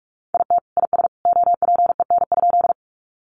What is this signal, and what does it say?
Morse code, heard as a keyed tone, about 45 wpm, 700 Hz. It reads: IT IS OPEN?
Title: Sentences from Top 300 Words 45wpm